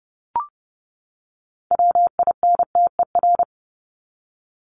Morse code, heard as a keyed tone, about 30 wpm, 700 Hz. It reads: WINTER
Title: Top 1501-1750 Words 30wpm